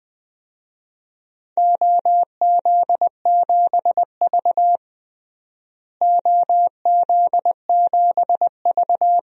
Morse code, heard as a keyed tone, about 20 wpm, 700 Hz. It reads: OZ7V OZ7V
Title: DXCC-100 Call Signs - 4 Characters - Repeated 3x - 20wpm